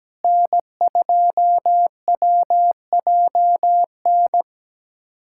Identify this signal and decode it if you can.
Morse code, heard as a keyed tone, about 17 wpm, 700 Hz. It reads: N2WJN
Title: USA Callsign DE Callsign - 5 Characters - 17wpm